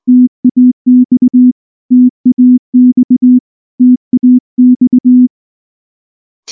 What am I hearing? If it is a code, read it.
Morse code, heard as a keyed tone, about 19 wpm, 250 Hz. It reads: TAX TAX TAX
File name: morse / morse-training-20WPM-10